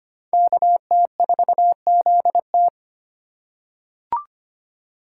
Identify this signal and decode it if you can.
Morse code, heard as a keyed tone, about 25 wpm, 700 Hz. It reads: KT4ZT
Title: USA Call Signs - 5 Characters - 3x Character Spacing - 25wpm